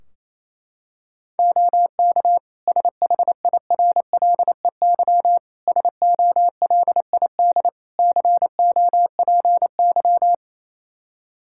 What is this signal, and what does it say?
Morse code, heard as a keyed tone, about 28 wpm, 700 Hz. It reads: OK SHIRLEY SOLID COPY
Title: Practice QSO - Exchange 3 - More Info - 28wpm